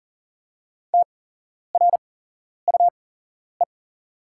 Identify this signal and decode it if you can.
Morse code, heard as a keyed tone, about 40 wpm, spaced as for 11 wpm, 690 Hz. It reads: TRUE
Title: 8x Character Spacing - Top 500 Words - 40wpm